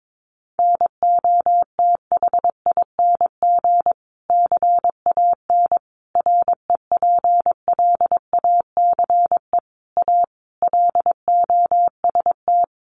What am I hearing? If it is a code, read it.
Morse code, heard as a keyed tone, about 22 wpm, 700 Hz. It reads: NOTHING CAN REPLACE A LOST
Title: Sentences from Top 1400 Words 22wpm